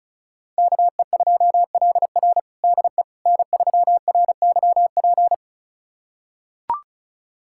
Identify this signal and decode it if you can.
Morse code, heard as a keyed tone, about 36 wpm, 700 Hz. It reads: KE2LR DE N3RYP
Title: USA Callsign DE Callsign - 5 Characters - 35wpm